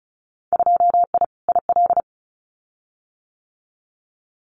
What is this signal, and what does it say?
Morse code, heard as a keyed tone, about 34 wpm, 700 Hz. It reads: 2I IL
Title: Winter Field Day Exchanges 35wpm